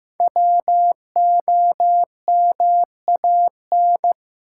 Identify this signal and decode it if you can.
Morse code, heard as a keyed tone, about 15 wpm, 700 Hz. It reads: WOMAN
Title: Sentences from Top 1100 Words - New Word Drill Alt - 15wpm